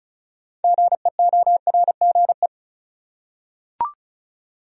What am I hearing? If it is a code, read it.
Morse code, heard as a keyed tone, about 34 wpm, 700 Hz. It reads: GEORGE
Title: USA Common Names - 35wpm